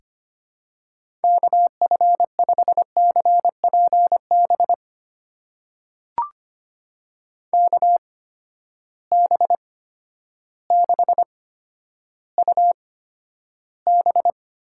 Morse code, heard as a keyed tone, about 25 wpm, spaced as for 11 wpm, 700 Hz. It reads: KF5CPB KB6UB
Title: USA Call Signs - 6 Characters - 8x Character Spacing - 25wpm